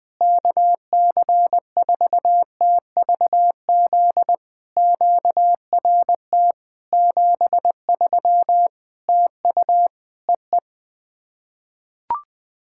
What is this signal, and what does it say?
Morse code, heard as a keyed tone, about 20 wpm, 700 Hz. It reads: KC4TVZ QRT 73 TU EE
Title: POTA QSO Practice - 20wpm